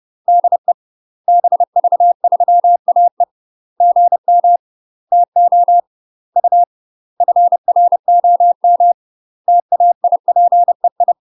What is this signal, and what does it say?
Morse code, heard as a keyed tone, about 30 wpm, 700 Hz. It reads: DE BV3AE GM TO U FROM TAIPEI